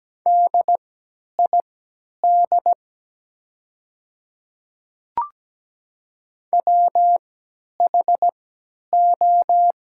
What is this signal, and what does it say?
Morse code, heard as a keyed tone, about 17 wpm, spaced as for 10 wpm, 700 Hz. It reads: DID WHO
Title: 3x Character Spacing - Top 100 Words - 17wpm